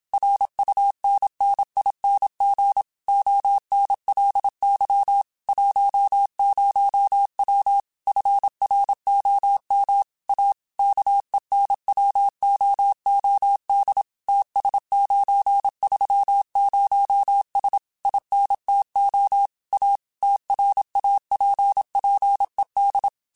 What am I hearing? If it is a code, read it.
Morse code, heard as a keyed tone, about 27 wpm, 800 Hz. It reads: RUNNING ONLY 10W FROM A KENWOOD TS930S INTO A TRAPPED
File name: QSOs / QSOs-mp3s / kc7sac@25